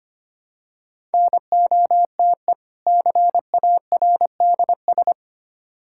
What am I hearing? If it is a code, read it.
Morse code, heard as a keyed tone, about 25 wpm, 700 Hz. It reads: NOTE CARDS
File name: practice / 25wpm / Difficult Words with Hints - 25wpm